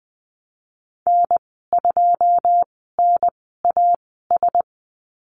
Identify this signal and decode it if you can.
Morse code, heard as a keyed tone, about 20 wpm, spaced as for 15 wpm, 700 Hz. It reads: N2NAS
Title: 088 Lesson 15g - Number 5 - Call Signs 20wpm